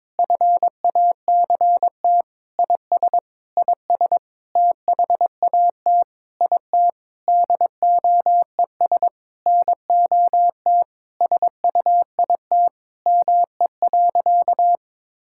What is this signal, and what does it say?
Morse code, heard as a keyed tone, about 22 wpm, 700 Hz. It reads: FACT IS IS THAT IT DOES NOT SUIT ME.
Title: Sentences from Top 1600 Words 22wpm